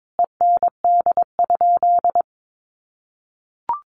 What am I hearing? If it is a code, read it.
Morse code, heard as a keyed tone, about 22 wpm, 700 Hz. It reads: END?
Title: Sentences from Top 1200 Words 22wpm